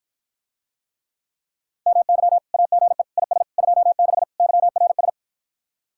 Morse code, heard as a keyed tone, about 52 wpm, 700 Hz. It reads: MY AGE IS 26 YRS